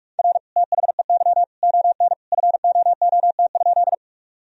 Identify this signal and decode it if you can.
Morse code, heard as a keyed tone, about 43 wpm, 700 Hz. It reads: R THEY ON FOOT?
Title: Sentences from Top 500 Words and Abbreviations 45wpm